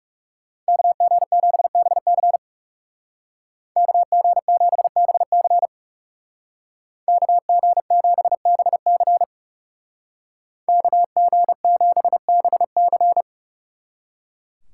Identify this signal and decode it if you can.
Morse code, heard as a keyed tone, about 38 wpm, 700 Hz. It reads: KG7BC KG7BC KG7BC KG7BC